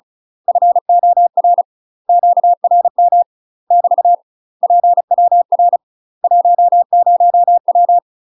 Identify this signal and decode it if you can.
Morse code, heard as a keyed tone, about 36 wpm, 700 Hz. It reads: FOR QRM <BT> PWR 10W
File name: UK MASTER QSOs TRAINING / 35WPM / UK MASTER QSOs 0012